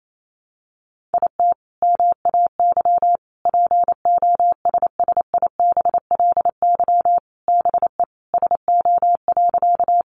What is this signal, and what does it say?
Morse code, heard as a keyed tone, about 28 wpm, 700 Hz. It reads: IT MAY POSSIBLY BE SO.
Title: Sentences from Top 1300 Words 28wpm